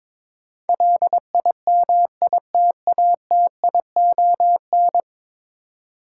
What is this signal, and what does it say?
Morse code, heard as a keyed tone, about 22 wpm, 700 Hz. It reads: LIMITATION
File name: practice / 22wpm / 10 Letter Words - First 3 Letters Given - 22wpm